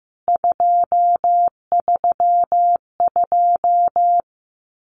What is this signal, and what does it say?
Morse code, heard as a keyed tone, about 15 wpm, 700 Hz. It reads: 232
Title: CWT Exchange 15wpm